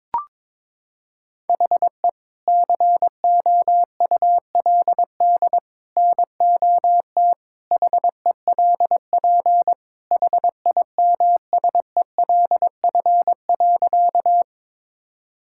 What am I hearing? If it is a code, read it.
Morse code, heard as a keyed tone, about 22 wpm, 700 Hz. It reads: HE COULD NOT HELP HIMSELF.
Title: Sentences from Top 1600 Words 22wpm